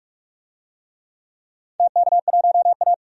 Morse code, heard as a keyed tone, about 44 wpm, 700 Hz. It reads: TK1A
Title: DXCC-100 Call Signs - 4 Characters - 45wpm